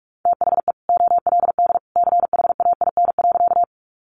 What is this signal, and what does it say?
Morse code, heard as a keyed tone, about 44 wpm, 700 Hz. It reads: THE OLD CHAIN.
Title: Sentences from Top 1500 Words 45wpm